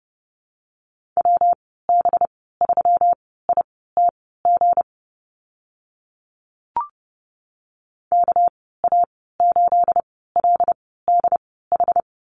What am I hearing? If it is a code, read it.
Morse code, heard as a keyed tone, about 30 wpm, spaced as for 17 wpm, 700 Hz. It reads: WB3ITG KA8LDH
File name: Lesson 37g - Letter x - Call Signs 30wpm